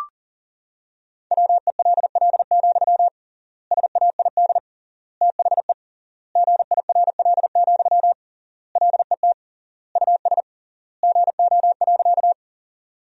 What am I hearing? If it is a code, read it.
Morse code, heard as a keyed tone, about 40 wpm, 700 Hz. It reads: WELL, SAID THE GIRL, LET US GO.
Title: 3x Word-Spacing - Sentences from Top 400 Words - 40wpm